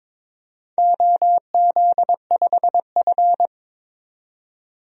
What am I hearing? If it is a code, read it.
Morse code, heard as a keyed tone, about 22 wpm, 700 Hz. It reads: OZ5F
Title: DXCC-100 Call Signs - 4 Characters - 22wpm